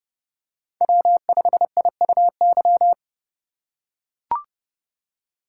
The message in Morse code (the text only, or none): W5IUY